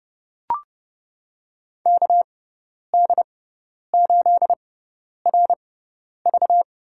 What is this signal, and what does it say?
Morse code, heard as a keyed tone, about 32 wpm, spaced as for 10 wpm, 700 Hz. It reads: KD8RV
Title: USA Call Signs - 6 Characters - 6x Character Spacing - 30wpm